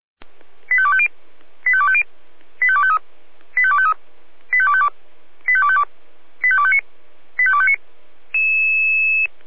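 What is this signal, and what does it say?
Five-tone selective call (ZVEI), heard as quick runs of five tones, tones of about 70 ms. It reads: 85250 85250 85253 85253 85252 85252 85259 85259
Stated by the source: ZVEI five-tone call 85250, received by SDR; from feuerwehr-orlamuende.org fire brigade server.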